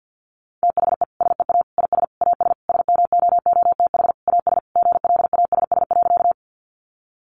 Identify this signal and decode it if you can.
Morse code, heard as a keyed tone, about 50 wpm, 700 Hz. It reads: THE SEA IS AS SMOOTH AS GLASS.
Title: Sentences from Top 1400 Words 50wpm